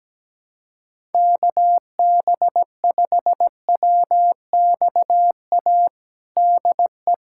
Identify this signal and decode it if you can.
Morse code, heard as a keyed tone, about 17 wpm, 700 Hz. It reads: KB5WXA DE
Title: USA Callsign DE Callsign - 6 Characters - 17wpm